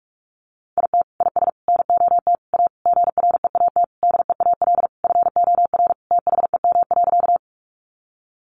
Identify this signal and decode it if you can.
Morse code, heard as a keyed tone, about 44 wpm, 700 Hz. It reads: IT IS NOT A GREAT DEAL FOR THEM.
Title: Sentences from Top 800 Words 45wpm